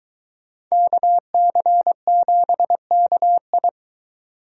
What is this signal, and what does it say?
Morse code, heard as a keyed tone, about 22 wpm, 700 Hz. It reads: KC7KI